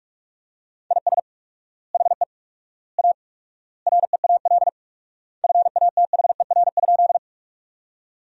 Morse code, heard as a keyed tone, about 42 wpm, 700 Hz. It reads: IS HE A REAL FATHER?
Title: 4x Word-Spacing - Sentences from Top 300 Words - 45wpm